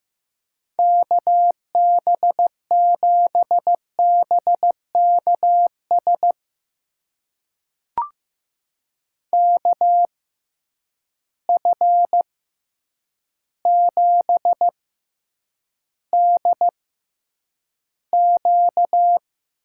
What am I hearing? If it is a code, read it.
Morse code, heard as a keyed tone, about 15 wpm, spaced as for 8 wpm, 700 Hz. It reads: KB7BKS KF7DQ
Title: USA Call Signs - 6 Characters - 6x Character Spacing - 15wpm